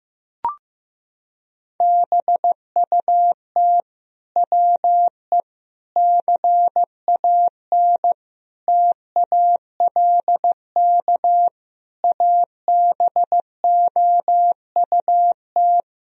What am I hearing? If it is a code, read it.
Morse code, heard as a keyed tone, about 15 wpm, 700 Hz. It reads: BUT WE CAN TALK ABOUT
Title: Sentences from Top 1000 Words - Review Part 4 - 15wpm